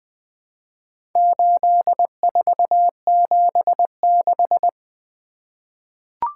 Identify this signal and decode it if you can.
Morse code, heard as a keyed tone, about 20 wpm, 700 Hz. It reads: 8476
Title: Four Letter-Number Combinations - 20wpm